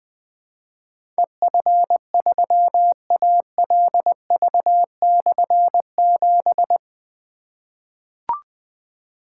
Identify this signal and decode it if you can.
Morse code, heard as a keyed tone, about 21 wpm, 700 Hz. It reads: EF3ALV/7
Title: Difficult Call Signs 20wpm